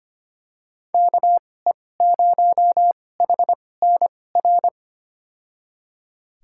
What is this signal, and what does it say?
Morse code, heard as a keyed tone, about 25 wpm, spaced as for 18 wpm, 700 Hz. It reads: KE0HNR
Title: USA Call Signs - 2x Character Spacing - 25wpm